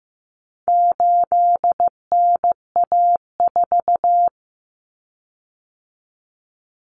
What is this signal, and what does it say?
Morse code, heard as a keyed tone, about 15 wpm, 700 Hz. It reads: 8NA4
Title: Common Cut Numbers - 4-Digit 15wpm